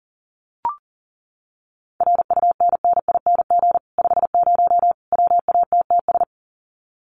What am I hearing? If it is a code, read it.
Morse code, heard as a keyed tone, about 40 wpm, 700 Hz. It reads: RUNNING 50 WATTS
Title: Common QSO Phrases 40wpm